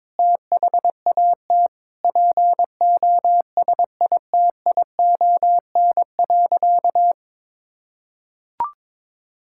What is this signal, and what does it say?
Morse code, heard as a keyed tone, about 22 wpm, 700 Hz. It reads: THAT POSITION.
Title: Sentences from Top 1000 Words - Review Part 5 - 22wpm